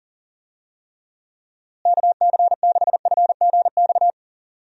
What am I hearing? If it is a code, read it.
Morse code, heard as a keyed tone, about 40 wpm, 700 Hz. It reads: KC6FGX
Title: USA Call Signs - 6 Characters - Repeated 3x - 40wpm